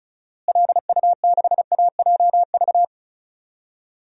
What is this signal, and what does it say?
Morse code, heard as a keyed tone, about 34 wpm, 700 Hz. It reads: LU6AJV
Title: DXCC-100 Call Signs - 6 Characters - Repeated 2x - 35wpm